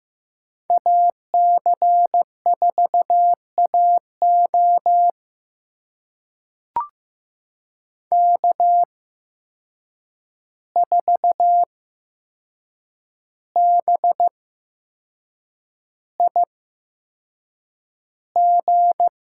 Morse code, heard as a keyed tone, about 15 wpm, spaced as for 6 wpm, 700 Hz. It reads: AC4AO K4BIG
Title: USA Call Signs - 5 Characters - 8x Character Spacing - 15wpm